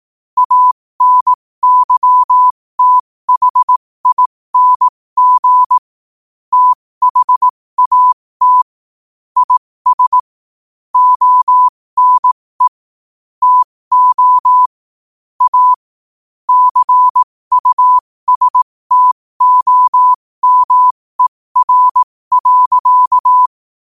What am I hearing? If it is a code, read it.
Morse code, heard as a keyed tone, about 18 wpm, spaced as for 15 wpm, 1000 Hz. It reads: ANYTHING THAT IS ONE TO A CUSTOMER.
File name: qotdfiles / 15wpm